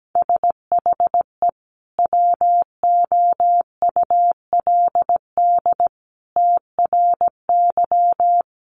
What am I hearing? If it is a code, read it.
Morse code, heard as a keyed tone, about 17 wpm, 700 Hz. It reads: SHE WOULD TRY
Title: Sentences from Top 600 Words 17wpm